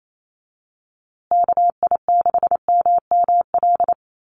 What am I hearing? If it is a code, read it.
Morse code, heard as a keyed tone, about 28 wpm, 700 Hz. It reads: KI6MML
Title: USA Call Signs Repeated 3x 28wpm